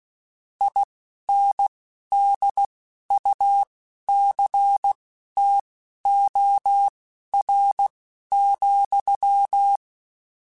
Morse code, heard as a keyed tone, about 16 wpm, spaced as for 11 wpm, 800 Hz. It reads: INDUCTOR,